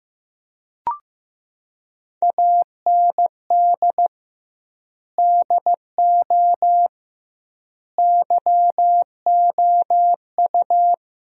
Morse code, heard as a keyed tone, about 15 wpm, 700 Hz. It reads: AND DO YOU
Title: 2x Word-Spacing - Sentences from Top 100 Words - 15wpm